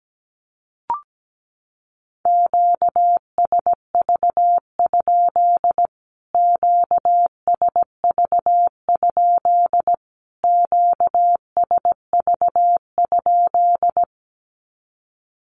Morse code, heard as a keyed tone, about 17 wpm, 700 Hz. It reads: QSV? QSV? QSV?
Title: Q-Codes Repeated 3x 17wpm